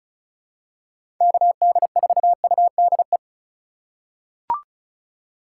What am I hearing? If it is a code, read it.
Morse code, heard as a keyed tone, about 34 wpm, 700 Hz. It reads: KD4UDE
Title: USA Call Signs - 3x Character Spacing - 35wpm